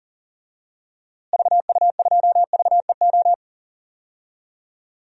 Morse code, heard as a keyed tone, about 40 wpm, 700 Hz. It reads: VU2VEO
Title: DXCC-100 Call Signs - 6 Characters - Repeated 2x - 40wpm